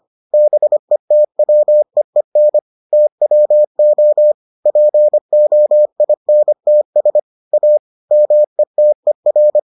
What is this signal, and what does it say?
Morse code, heard as a keyed tone, about 25 wpm, 600 Hz. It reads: BETWEEN TWO POINTS A METER